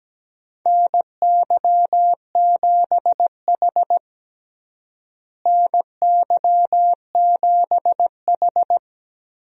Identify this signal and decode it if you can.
Morse code, heard as a keyed tone, about 17 wpm, 700 Hz. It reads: NY7H NY7H
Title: USA Call Signs - 4 Characters - Repeated 2x - 17wpm